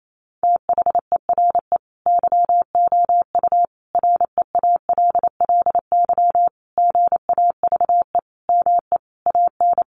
Morse code, heard as a keyed tone, about 28 wpm, 700 Hz. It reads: THERE YOU REALLY GAVE ME AN
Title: Sentences from Top 900 Words 28wpm